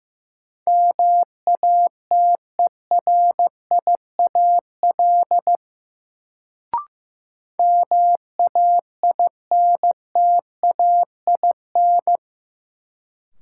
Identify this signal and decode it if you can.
Morse code, heard as a keyed tone, about 15 wpm, 700 Hz. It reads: MATERIAL MAINTAIN
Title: Sentences from Top 1000 Words - Review Part 5 - New Word Drill - 15wpm